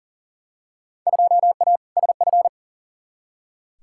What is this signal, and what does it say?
Morse code, heard as a keyed tone, about 40 wpm, 700 Hz. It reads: 2A SF
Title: Field Day Exchanges - 40wpm